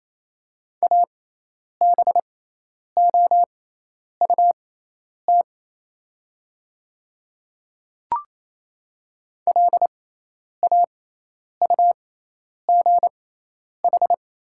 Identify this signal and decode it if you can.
Morse code, heard as a keyed tone, about 28 wpm, spaced as for 10 wpm, 700 Hz. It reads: ABOUT LAUGH
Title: 6x Character Spacing - Top 500 Words - 28wpm